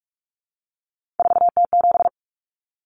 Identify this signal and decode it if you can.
Morse code, heard as a keyed tone, about 44 wpm, 700 Hz. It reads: 4T7
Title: Common Cut Numbers - 3-Digit 45wpm